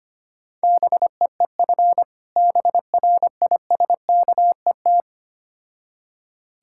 Morse code, heard as a keyed tone, about 25 wpm, 700 Hz. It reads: BEEF BRISKET
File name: practice / 25wpm / Difficult Words with Hints - 25wpm